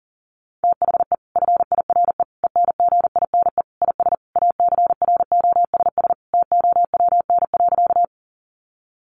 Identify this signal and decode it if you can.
Morse code, heard as a keyed tone, about 40 wpm, 700 Hz. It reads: THE FIRE ENGINE IS ACROSS TOWN.
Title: Sentences from Top 1500 Words 40wpm